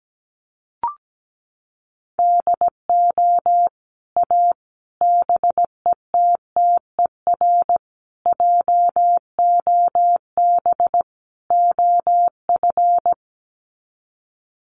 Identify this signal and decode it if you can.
Morse code, heard as a keyed tone, about 17 wpm, 700 Hz. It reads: DO A BETTER JOB OF
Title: Sets of 5 Words Encore2 17wpm